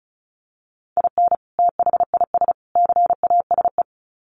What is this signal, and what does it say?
Morse code, heard as a keyed tone, about 34 wpm, 700 Hz. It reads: IN THIS CASE
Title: Sets of 3 Words 35wpm